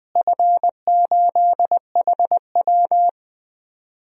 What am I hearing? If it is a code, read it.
Morse code, heard as a keyed tone, about 21 wpm, 700 Hz. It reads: F8HW
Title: DXCC-100 Call Signs - 4 Characters - Repeated 3x - 20wpm